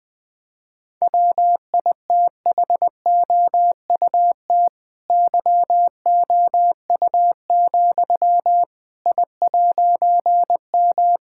Morse code, heard as a keyed tone, about 20 wpm, 700 Hz. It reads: WITHOUT YOU, I'M
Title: All-Age Appropriate Jokes - 20wpm